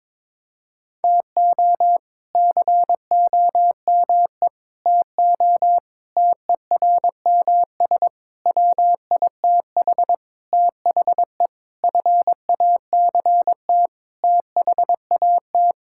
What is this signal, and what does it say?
Morse code, heard as a keyed tone, about 22 wpm, 700 Hz. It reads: TO COME TO TERMS WITH THE FACT THAT
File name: practice / 22wpm / Sentences with 8-Word Phrases - 22wpm